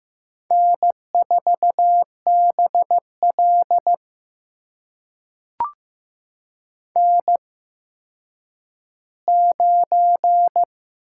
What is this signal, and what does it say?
Morse code, heard as a keyed tone, about 15 wpm, spaced as for 9 wpm, 700 Hz. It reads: N4BL N9
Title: USA Call Signs - 4 Characters - 8x Character Spacing - 15wpm